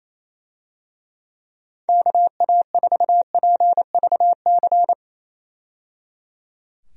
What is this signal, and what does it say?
Morse code, heard as a keyed tone, about 28 wpm, 700 Hz. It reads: KA4PVC
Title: USA Call Signs - 6 Characters - 28wpm